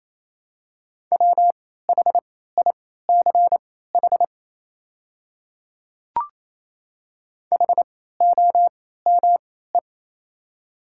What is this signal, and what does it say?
Morse code, heard as a keyed tone, about 27 wpm, spaced as for 16 wpm, 700 Hz. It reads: WHICH HOME
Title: 3x Character Spacing - Top 200 Words - 28wpm